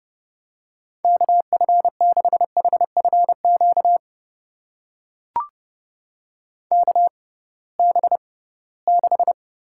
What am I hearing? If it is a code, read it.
Morse code, heard as a keyed tone, about 30 wpm, spaced as for 19 wpm, 700 Hz. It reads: KF6HFQ KB6
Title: USA Call Signs - 6 Characters - 6x Character Spacing - 30wpm